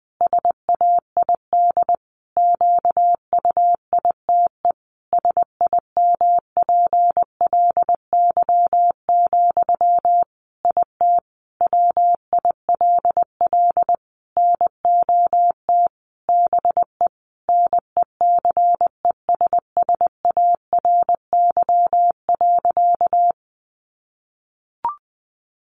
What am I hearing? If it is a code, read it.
Morse code, heard as a keyed tone, about 20 wpm, 700 Hz. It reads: SAID QUITE SIMPLY, IT WILL NOT BE NECESSARY.
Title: Sentences from Top 800 Words 20wpm